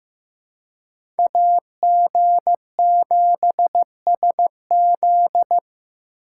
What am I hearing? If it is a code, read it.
Morse code, heard as a keyed tone, about 15 wpm, 700 Hz. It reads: AG7SZ
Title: USA Call Signs - 6x Character Spacing - 15wpm